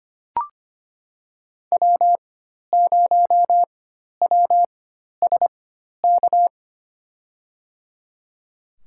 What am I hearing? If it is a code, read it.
Morse code, heard as a keyed tone, about 25 wpm, spaced as for 12 wpm, 700 Hz. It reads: W0WSK
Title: USA Call Signs - 5 Characters - 4x Character Spacing - 25wpm